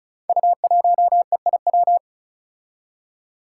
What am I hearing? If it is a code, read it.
Morse code, heard as a keyed tone, about 36 wpm, 700 Hz. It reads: U1EIW
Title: Five Challenging Letter-Number Combinations 35wpm